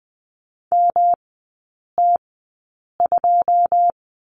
Morse code, heard as a keyed tone, about 20 wpm, spaced as for 8 wpm, 700 Hz. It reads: MT2
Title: 107 Lesson 18e - Letter m - Three Characters 20wpm